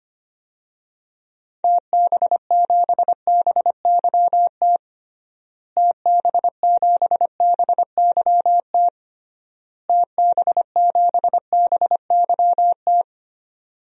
Morse code, heard as a keyed tone, about 25 wpm, 700 Hz. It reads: TB7BYT TB7BYT TB7BYT
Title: DXCC-100 Call Signs - 6 Characters - Repeated 3x - 25wpm